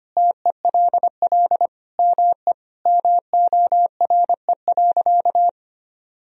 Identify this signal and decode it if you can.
Morse code, heard as a keyed tone, about 25 wpm, 700 Hz. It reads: TELL ME MORE.
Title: Sentences from Top 1000 Words - Review Part 1 - 25wpm